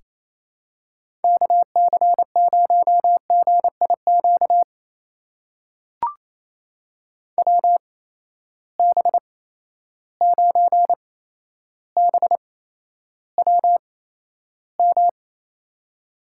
Morse code, heard as a keyed tone, about 28 wpm, spaced as for 12 wpm, 700 Hz. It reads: KC0GIQ WB9BWM
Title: USA Call Signs - 6 Characters - 8x Character Spacing - 28wpm